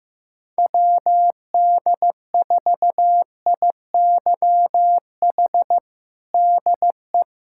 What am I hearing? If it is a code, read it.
Morse code, heard as a keyed tone, about 15 wpm, 700 Hz. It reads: WD4IYH DE
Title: Practice QSO - Exchange 4 - Ending - 15wpm